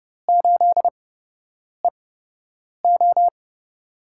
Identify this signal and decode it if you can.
Morse code, heard as a keyed tone, about 32 wpm, spaced as for 8 wpm, 700 Hz. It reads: 8EO
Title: USA Call Signs - 8x Character Spacing - 30wpm